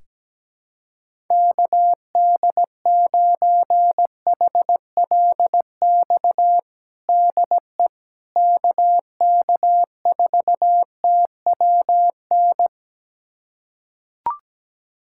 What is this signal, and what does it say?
Morse code, heard as a keyed tone, about 17 wpm, 700 Hz. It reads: KD9HLX DE KK4TWN
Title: USA Callsign DE Callsign - 6 Characters - 17wpm